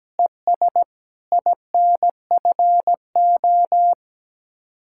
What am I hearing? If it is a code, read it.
Morse code, heard as a keyed tone, about 17 wpm, 700 Hz. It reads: ES INFO
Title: Speed-Racing Common QSO Phrases Encore - 17wpm